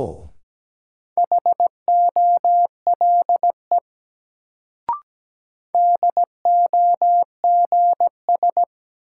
Morse code, heard as a keyed tone, about 17 wpm, 700 Hz. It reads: HOLE DOGS
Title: Sentences from Top 1600 Words - New Word Drill - 17wpm